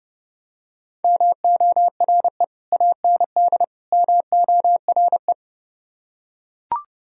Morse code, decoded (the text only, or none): MORE AND MORE